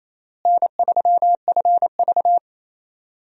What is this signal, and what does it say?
Morse code, heard as a keyed tone, about 28 wpm, 700 Hz. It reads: N3FV
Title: USA Call Signs - 4 Characters - Repeated 3x - 28wpm